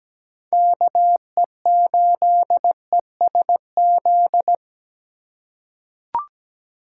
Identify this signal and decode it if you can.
Morse code, heard as a keyed tone, about 17 wpm, 700 Hz. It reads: KE8ESZ
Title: USA Call Signs - 6 Characters - 2x Character Spacing - 17wpm